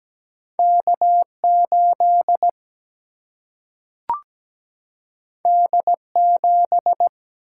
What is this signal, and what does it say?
Morse code, heard as a keyed tone, about 17 wpm, 700 Hz. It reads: K8 D7
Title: Single-Letter Single-Number 17wpm